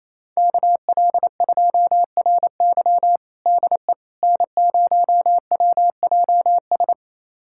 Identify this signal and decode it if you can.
Morse code, heard as a keyed tone, about 28 wpm, 700 Hz. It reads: KL2RY DE N0WJS
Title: USA Callsign DE Callsign - 5 Characters - 28wpm